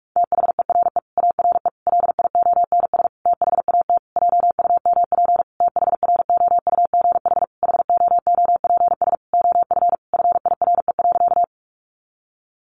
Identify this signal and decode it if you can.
Morse code, heard as a keyed tone, about 44 wpm, 700 Hz. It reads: THERE ARE LIONS THAT JUMP THROUGH HOOPS OF FIRE.
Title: Easy Phrases 45wpm